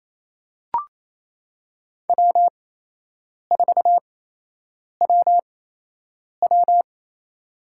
Morse code, heard as a keyed tone, about 28 wpm, spaced as for 8 wpm, 700 Hz. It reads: W4WW
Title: USA Call Signs - 4 Characters - 8x Character Spacing - 28wpm